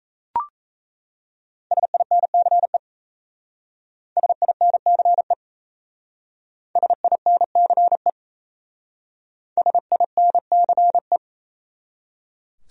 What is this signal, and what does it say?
Morse code, heard as a keyed tone, about 33 wpm, 700 Hz. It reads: SINCE SINCE SINCE SINCE
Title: Speed-Racing Top 500 Words 28wpm